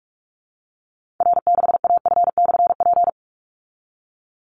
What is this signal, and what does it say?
Morse code, heard as a keyed tone, about 44 wpm, 700 Hz. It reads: R6AF/P
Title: Call Signs Repeated 2x - 45wpm